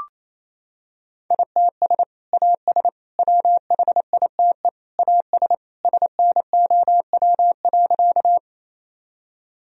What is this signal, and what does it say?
Morse code, heard as a keyed tone, about 27 wpm, 700 Hz. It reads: ITS AS WHITE AS SNOW.
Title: Sentences from Top 1000 Words - Review Part 3 - 28wpm